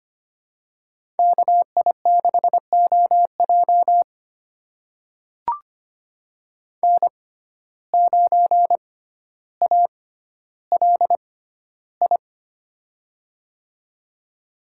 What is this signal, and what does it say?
Morse code, heard as a keyed tone, about 25 wpm, spaced as for 13 wpm, 700 Hz. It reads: KI6OJ N9ALI
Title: USA Call Signs - 5 Characters - 6x Character Spacing - 25wpm